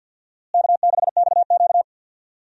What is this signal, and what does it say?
Morse code, heard as a keyed tone, about 52 wpm, 700 Hz. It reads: K6XQ
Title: USA Call Signs - 4 Characters - Repeated 3x - 50wpm